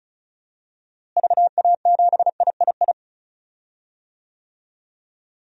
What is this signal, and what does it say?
Morse code, heard as a keyed tone, about 34 wpm, 700 Hz. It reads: VA7III
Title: DXCC-100 Call Signs - 6 Characters - Repeated 3x - 35wpm